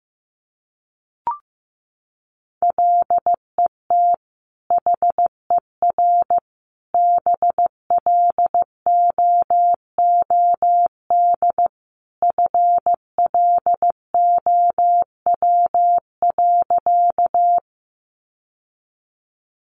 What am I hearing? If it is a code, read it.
Morse code, heard as a keyed tone, about 15 wpm, 700 Hz. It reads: LET HER BLOOD FLOW.
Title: Sentences from Top 1200 Words 15wpm